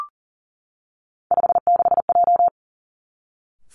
Morse code, heard as a keyed tone, about 41 wpm, 690 Hz. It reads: 56J
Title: Three Challenging Letter-Number Combinations 40wpm